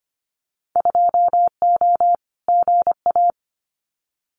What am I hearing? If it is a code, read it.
Morse code, heard as a keyed tone, about 25 wpm, 700 Hz. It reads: 2O GA
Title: Winter Field Day Exchanges Encore 25wpm